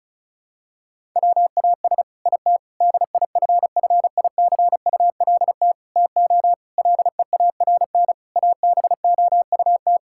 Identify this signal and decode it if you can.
Morse code, heard as a keyed tone, about 34 wpm, 700 Hz. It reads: WAS IT DIFFICULT TO LEARN ABOUT
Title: Sentences from Top 1000 Words - Review Part 6 - 35wpm